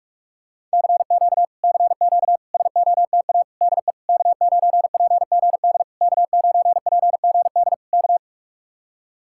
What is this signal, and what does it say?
Morse code, heard as a keyed tone, about 49 wpm, 700 Hz. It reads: CQ CQ SOTA DE K9PGD K9PGD K